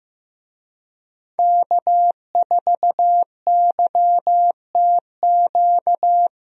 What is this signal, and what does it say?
Morse code, heard as a keyed tone, about 15 wpm, 700 Hz. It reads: K4YTQ
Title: USA Call Signs - 5 Characters - Repeated 3x - 15wpm